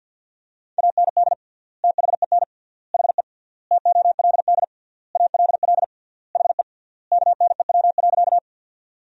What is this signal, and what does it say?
Morse code, heard as a keyed tone, about 52 wpm, 700 Hz. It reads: AND THEN HE TOLD ALL HE KNEW.
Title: 3x Word-Spacing - Sentences from Top 400 Words - 50wpm